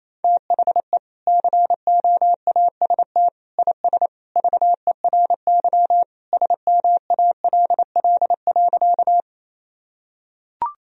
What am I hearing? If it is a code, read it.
Morse code, heard as a keyed tone, about 28 wpm, 700 Hz. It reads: THE COAST IS VERY SMALL.
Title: Sentences from Top 1000 Words - Review Part 4 - 28wpm